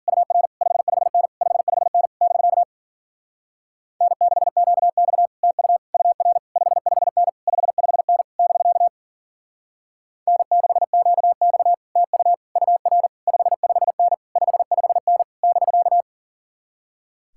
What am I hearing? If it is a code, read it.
Morse code, heard as a keyed tone, about 51 wpm, 700 Hz. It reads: UR 55N 55N <BK> N6QX TU UR 55N 55N <BK> N6QX TU UR 55N 55N <BK>